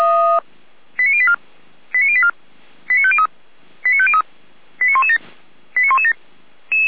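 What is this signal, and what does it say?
Five-tone selective call (ZVEI), heard as quick runs of five tones, tones of about 70 ms. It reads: 89084 89084 89503 89503 89117 89117
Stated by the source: ZVEI five-tone selective call received on an SDR.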